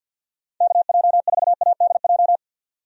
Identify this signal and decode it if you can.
Morse code, heard as a keyed tone, about 50 wpm, 700 Hz. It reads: KJ4ADJ
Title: USA Call Signs - 6 Characters - 50wpm